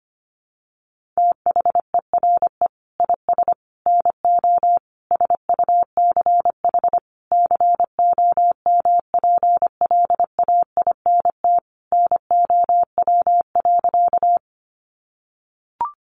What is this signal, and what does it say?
Morse code, heard as a keyed tone, about 25 wpm, 700 Hz. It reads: THERE IS NO SUCH COMPLAINT NOW.
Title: Sentences from Top 1400 Words 25wpm